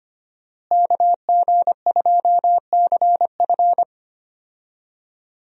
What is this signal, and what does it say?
Morse code, heard as a keyed tone, about 25 wpm, 700 Hz. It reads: KG2CF
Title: USA Call Signs - 5 Characters - Repeated 3x - 25wpm